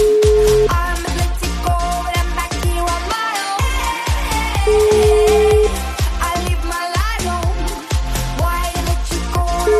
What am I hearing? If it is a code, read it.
Ringback tone (Europe, 425 Hz), with noise.